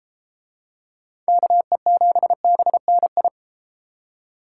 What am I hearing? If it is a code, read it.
Morse code, heard as a keyed tone, about 32 wpm, 700 Hz. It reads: KE7BNI